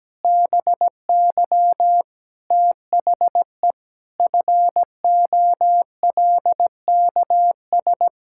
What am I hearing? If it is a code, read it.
Morse code, heard as a keyed tone, about 17 wpm, 700 Hz. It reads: BY THE FOLKS
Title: Pangram Challenge v2 - 17wpm